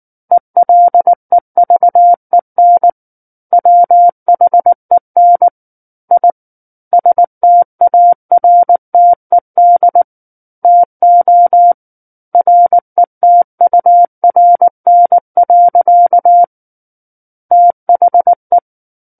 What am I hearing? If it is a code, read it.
Morse code, heard as a keyed tone, about 19 wpm, 700 Hz. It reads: ELEVEN WHEN I STARTED TO RETURN. THE
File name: Book One / 10-Chapter 10 - 19 WPM